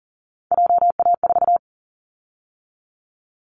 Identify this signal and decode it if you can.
Morse code, heard as a keyed tone, about 40 wpm, 700 Hz. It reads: JA4
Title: Callsign Prefixes for Top 25 DXCC - 40wpm